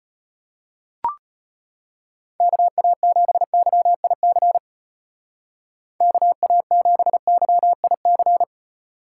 Morse code, heard as a keyed tone, about 31 wpm, 700 Hz. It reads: KA7YIC KA7YIC